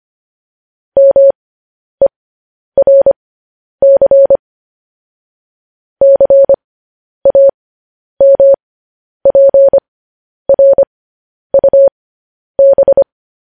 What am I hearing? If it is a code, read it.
Morse code, heard as a keyed tone, about 25 wpm, spaced as for 10 wpm, 550 Hz. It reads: MERC CAMPRUB
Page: news headlines in morse code at 10 wpm